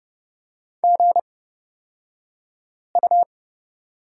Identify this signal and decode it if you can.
Morse code, heard as a keyed tone, about 30 wpm, spaced as for 5 wpm, 690 Hz. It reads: GU
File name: Lesson 36d - BT - Two Characters 30wpm